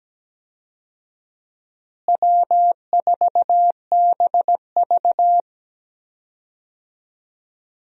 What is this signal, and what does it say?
Morse code, heard as a keyed tone, about 17 wpm, 700 Hz. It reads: W4BV
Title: USA Call Signs - 4 Characters - Repeated 2x - 17wpm